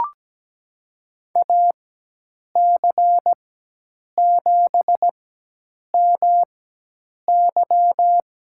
Morse code, heard as a keyed tone, about 17 wpm, spaced as for 8 wpm, 700 Hz. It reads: AC7MY
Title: USA Call Signs - 5 Characters - 4x Character Spacing - 17wpm